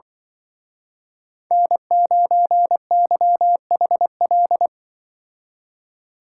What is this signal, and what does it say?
Morse code, heard as a keyed tone, about 24 wpm, 700 Hz. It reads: N9YHL
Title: Speed Racing - USA Call Signs 20wpm